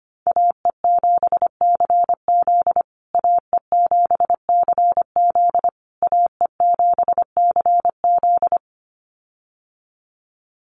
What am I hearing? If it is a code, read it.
Morse code, heard as a keyed tone, about 25 wpm, 700 Hz. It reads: AE7CZ AE7CZ AE7CZ
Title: USA Call Signs Repeated 3x 25wpm